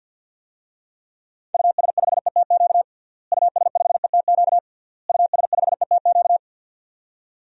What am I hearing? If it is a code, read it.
Morse code, heard as a keyed tone, about 47 wpm, 700 Hz. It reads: US5ETQ US5ETQ US5ETQ
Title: DXCC-100 Call Signs - 6 Characters - Repeated 3x - 50wpm